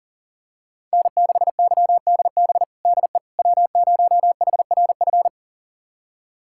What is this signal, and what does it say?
Morse code, heard as a keyed tone, about 40 wpm, 700 Hz. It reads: N6YDB DE W0HRF
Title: USA Callsign DE Callsign - 5 Characters - 40wpm